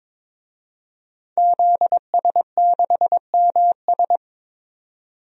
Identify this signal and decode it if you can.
Morse code, heard as a keyed tone, about 21 wpm, 700 Hz. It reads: ZS6MS
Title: DXCC-100 Call Signs - 5 Characters - 22wpm